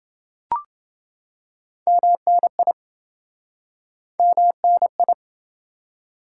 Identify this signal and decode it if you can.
Morse code, heard as a keyed tone, about 28 wpm, 700 Hz. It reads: MNI MNI